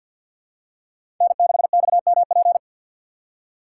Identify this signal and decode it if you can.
Morse code, heard as a keyed tone, about 52 wpm, 700 Hz. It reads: N6XMP
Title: USA Call Signs - 4x Character Spacing - 50wpm